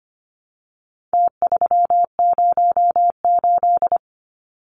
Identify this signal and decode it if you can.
Morse code, heard as a keyed tone, about 25 wpm, 700 Hz. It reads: T308
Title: Numbers - Contest Style - 4-Digit 25wpm